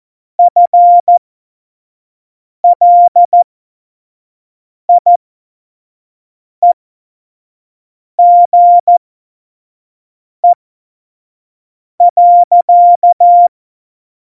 Morse code, heard as a keyed tone, about 14 wpm, spaced as for 5 wpm, 700 Hz. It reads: FLIEGE.
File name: Morsekurs / Witze / Witz 03. September 2019, 25 F70